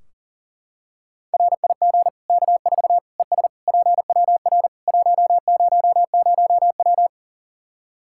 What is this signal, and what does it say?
Morse code, heard as a keyed tone, about 37 wpm, 700 Hz. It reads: RIG K4 ES PWR 100W
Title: Practice QSO - Exchange 2 - Info - 40wpm